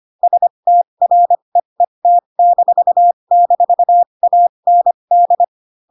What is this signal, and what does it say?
Morse code, heard as a keyed tone, about 25 wpm, spaced as for 22 wpm, 700 Hz. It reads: STREET--AND